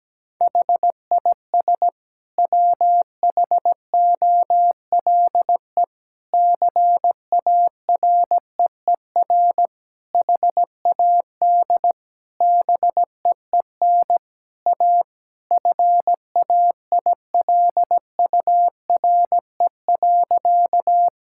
Morse code, heard as a keyed tone, about 17 wpm, 700 Hz. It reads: HIS WHOLE CAREER HAD BEEN A FAILURE.
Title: Sentences from Top 1000 Words - Review Part 6 - 17wpm